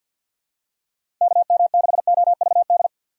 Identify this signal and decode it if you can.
Morse code, heard as a keyed tone, about 51 wpm, 690 Hz. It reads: KM6OVD
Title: USA Call Signs - 4x Character Spacing - 50wpm